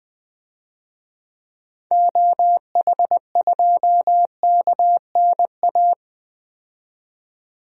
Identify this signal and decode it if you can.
Morse code, heard as a keyed tone, about 20 wpm, 700 Hz. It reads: OH2KNA
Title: DXCC-100 Call Signs - 6 Characters - 20wpm